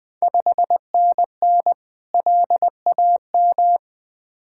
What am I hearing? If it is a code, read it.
Morse code, heard as a keyed tone, about 21 wpm, 700 Hz. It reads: 5NN LAM
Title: Ontario QSO Party Exchange 20wpm